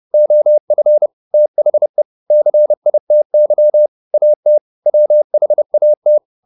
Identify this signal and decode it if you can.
Morse code, heard as a keyed tone, about 30 wpm, 600 Hz. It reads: OF THE CITY AT WHAT